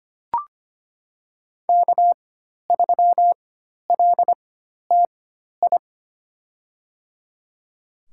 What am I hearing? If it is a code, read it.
Morse code, heard as a keyed tone, about 25 wpm, spaced as for 12 wpm, 700 Hz. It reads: K3LTI
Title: USA Call Signs - 5 Characters - 4x Character Spacing - 25wpm